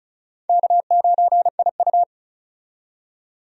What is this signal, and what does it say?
Morse code, heard as a keyed tone, about 34 wpm, 700 Hz. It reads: K9IU
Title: USA Call Signs - 4 Characters - 35wpm